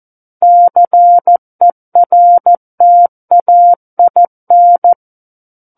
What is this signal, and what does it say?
Morse code, heard as a keyed tone, about 14 wpm, 700 Hz. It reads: CERTAIN
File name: Book One / 05-Chapter 5 - 14 WPM